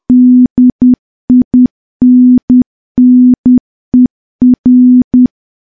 Morse code, heard as a keyed tone, about 10 wpm, 250 Hz. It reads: DINNER
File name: morse / morse-training-10WPM